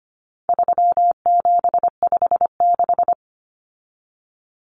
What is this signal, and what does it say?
Morse code, heard as a keyed tone, about 25 wpm, 700 Hz. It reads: 3756
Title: Common Cut Numbers - 4-Digit 25wpm